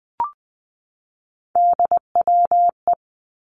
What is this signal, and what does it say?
Morse code, heard as a keyed tone, about 20 wpm, 700 Hz. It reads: DWE
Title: Three Challenging Letter-Number Combinations 20wpm